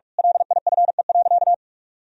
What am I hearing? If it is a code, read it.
Morse code, heard as a keyed tone, about 46 wpm, 700 Hz. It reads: LIFE.
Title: Sentences from Top 1000 Words - Review Part 3 - 45wpm